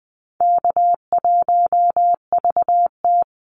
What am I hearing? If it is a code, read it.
Morse code, heard as a keyed tone, about 20 wpm, 700 Hz. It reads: K1VT
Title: Call Signs - 20wpm